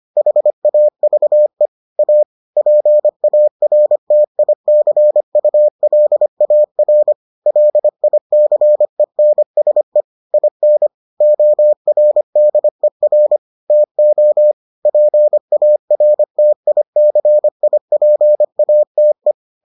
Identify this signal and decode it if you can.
Morse code, heard as a keyed tone, about 25 wpm, 600 Hz. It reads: HAVE A PARTICULAR LICENSE IN ORDER TO PARTICIPATE